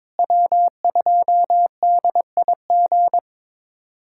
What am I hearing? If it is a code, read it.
Morse code, heard as a keyed tone, about 22 wpm, 700 Hz. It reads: W2DIG
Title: USA Callsign DE Callsign - 5 Characters - 22wpm